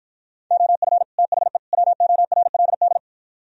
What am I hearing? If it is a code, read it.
Morse code, heard as a keyed tone, about 52 wpm, 700 Hz. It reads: OF THE WORLD